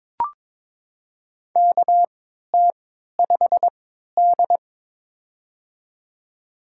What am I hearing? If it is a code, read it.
Morse code, heard as a keyed tone, about 22 wpm, spaced as for 13 wpm, 700 Hz. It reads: KT5D
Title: USA Call Signs - 4 Characters - 3x Character Spacing - 22wpm